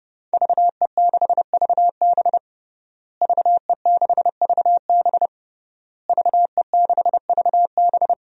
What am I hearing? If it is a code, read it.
Morse code, heard as a keyed tone, about 30 wpm, 700 Hz. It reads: VE6VB VE6VB VE6VB
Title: DXCC-100 Call Signs - 5 Characters - Repeated 3x - 30wpm